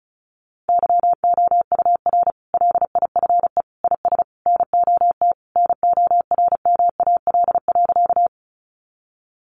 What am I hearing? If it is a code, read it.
Morse code, heard as a keyed tone, about 34 wpm, 700 Hz. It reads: YOUR LIFE IS NOT NORMAL.
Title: Sentences from Top 1200 Words 35wpm